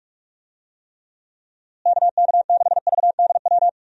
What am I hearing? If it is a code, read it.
Morse code, heard as a keyed tone, about 43 wpm, 700 Hz. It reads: KK6VDW
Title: USA Call Signs - 6 Characters - Repeated 3x - 45wpm